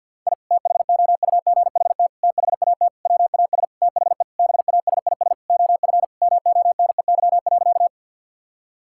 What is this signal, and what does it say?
Morse code, heard as a keyed tone, about 49 wpm, 700 Hz. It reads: I THOUGHT THAT WAS THE BASIS OF MONEY.
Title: Sentences from Top 1000 Words - Review Part 5 - 50wpm